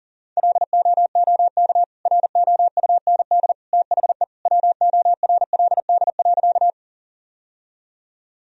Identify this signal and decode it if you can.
Morse code, heard as a keyed tone, about 40 wpm, 700 Hz. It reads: LOOK ROUND THE WORLD.
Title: Sentences from Top 1000 Words - Review Part 1 - 40wpm